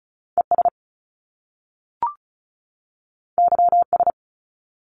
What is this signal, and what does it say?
Morse code, heard as a keyed tone, about 36 wpm, 700 Hz. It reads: ES YS
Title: Top Two Letter Combinations in Words 35wpm